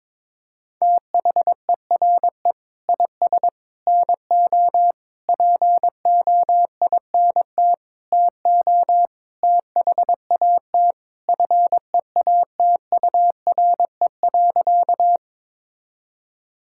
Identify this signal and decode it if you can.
Morse code, heard as a keyed tone, about 22 wpm, 700 Hz. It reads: THERE IS NO POINT TO THAT FEATURE.
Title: Sentences from Top 1000 Words - Review Part 5 - 22wpm